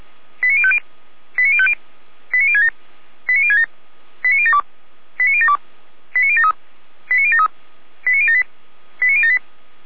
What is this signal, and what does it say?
Five-tone selective call (ZVEI), heard as quick runs of five tones, tones of about 70 ms. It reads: 89050 89050 89067 89067 89072 89072 89073 89073 89078 89078